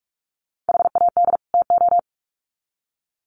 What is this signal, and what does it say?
Morse code, heard as a keyed tone, about 44 wpm, 700 Hz. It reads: HAD TO
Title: Sets of 2 Words 45wpm